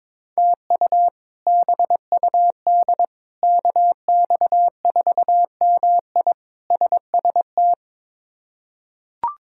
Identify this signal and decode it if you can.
Morse code, heard as a keyed tone, about 22 wpm, 700 Hz. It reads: TU BUD KX4MI SST
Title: K1USN SST QSO Practice - 22wpm